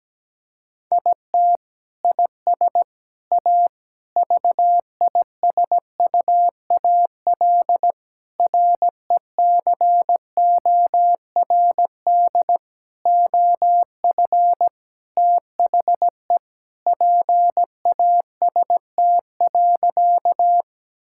Morse code, heard as a keyed tone, about 17 wpm, 700 Hz. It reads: IT IS A VISUAL RECORD OF THE PAST.
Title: Sentences from Top 1000 Words - Review Part 6 - 17wpm